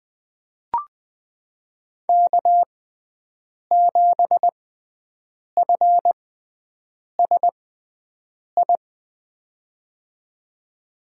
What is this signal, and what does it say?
Morse code, heard as a keyed tone, about 21 wpm, spaced as for 7 wpm, 700 Hz. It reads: K7FSI
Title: USA Call Signs - 6x Character Spacing - 20wpm